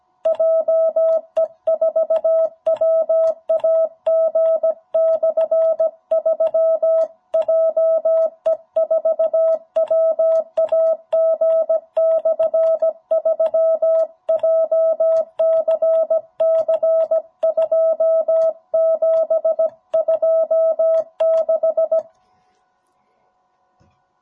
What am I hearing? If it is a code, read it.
Morse code, heard as a keyed tone, about 17 wpm, 650 Hz. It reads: JE4WAG/3 JE4WAG/3 JCC 2726
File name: jh8cbh / 2021cw